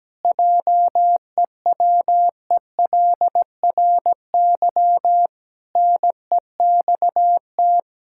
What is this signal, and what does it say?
Morse code, heard as a keyed tone, about 17 wpm, 700 Hz. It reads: JEWELRY NEXT
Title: Pangram Challenge v2 - 17wpm